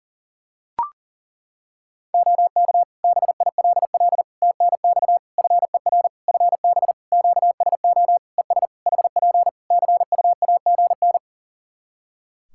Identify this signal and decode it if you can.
Morse code, heard as a keyed tone, about 37 wpm, 700 Hz. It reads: OK BILL TNX FER FB QSO ES HP CUAGN
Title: Practice QSO - Exchange 4 - Ending - 40wpm